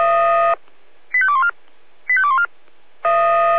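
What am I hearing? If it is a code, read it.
Five-tone selective call (ZVEI), heard as quick runs of five tones, tones of about 70 ms. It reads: 85214 85214
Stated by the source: ZVEI five-tone selective call 85217 received on an SDR.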